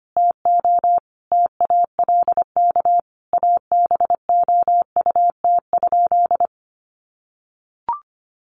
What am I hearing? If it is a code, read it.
Morse code, heard as a keyed tone, about 25 wpm, 700 Hz. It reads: TO TALK ABOUT?
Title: Sentences from Top 1100 Words 25wpm